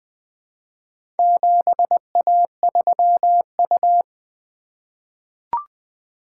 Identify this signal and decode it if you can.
Morse code, heard as a keyed tone, about 20 wpm, 700 Hz. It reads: 7A3U
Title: Four Letter-Number Combinations - 20wpm